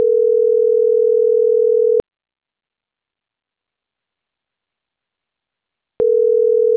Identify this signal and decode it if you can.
Ringback tone (North America), 440 + 480 Hz.